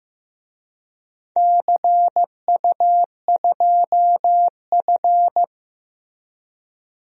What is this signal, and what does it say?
Morse code, heard as a keyed tone, about 15 wpm, 700 Hz. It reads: CU2F
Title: DXCC-100 Call Signs - 4 Characters - Repeated 3x - 15wpm